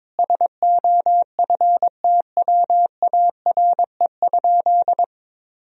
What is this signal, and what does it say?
Morse code, heard as a keyed tone, about 21 wpm, 700 Hz. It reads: SOFTWARE?
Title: Sentences from Top 1000 Words - Review Part 3 - 22wpm